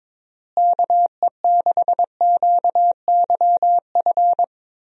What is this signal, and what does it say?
Morse code, heard as a keyed tone, about 22 wpm, 700 Hz. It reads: KE6QYF
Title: USA Call Signs - 6 Characters - Repeated 3x - 22wpm